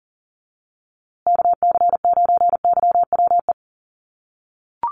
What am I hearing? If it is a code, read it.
Morse code, heard as a keyed tone, about 40 wpm, 700 Hz. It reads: KC9YWE
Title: USA Call Signs Repeated 3x 40wpm